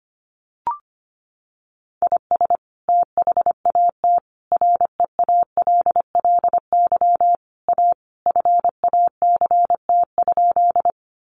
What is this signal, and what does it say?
Morse code, heard as a keyed tone, about 25 wpm, 700 Hz. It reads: IS THAT REALLY A FACT?
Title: Sentences from Top 600 Words 25wpm